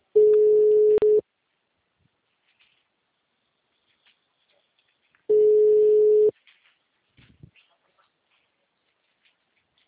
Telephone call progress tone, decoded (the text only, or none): ringback tone